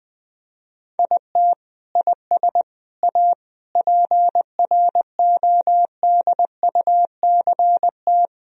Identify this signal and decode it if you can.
Morse code, heard as a keyed tone, about 21 wpm, 700 Hz. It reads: IT IS A PRODUCT
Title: Sentences from Top 1000 Words - Review Part 4 - 20wpm